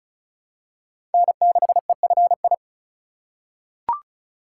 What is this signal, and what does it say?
Morse code, heard as a keyed tone, about 34 wpm, 700 Hz. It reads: N6EFI
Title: USA Call Signs - 5 Characters - 4x Character Spacing - 35wpm